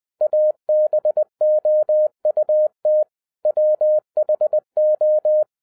Morse code, heard as a keyed tone, about 20 wpm, 610 Hz. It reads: ABOUT WHO